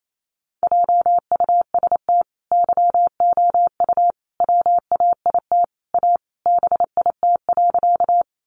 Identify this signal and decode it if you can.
Morse code, heard as a keyed tone, about 28 wpm, 700 Hz. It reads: JUST YOU WAIT A BIT.
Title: Sentences from Top 900 Words 28wpm